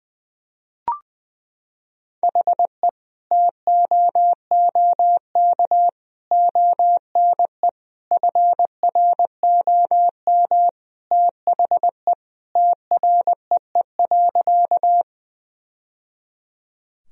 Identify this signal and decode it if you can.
Morse code, heard as a keyed tone, about 21 wpm, 700 Hz. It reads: HE TOOK ONE FROM THE TREE.
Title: Sentences from Top 1000 Words - Review Part 1 - 20wpm